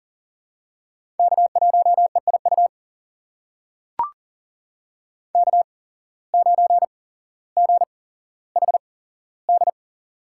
Morse code, heard as a keyed tone, about 40 wpm, spaced as for 17 wpm, 700 Hz. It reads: K1EIU K9GHD
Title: USA Call Signs - 5 Characters - 8x Character Spacing - 40wpm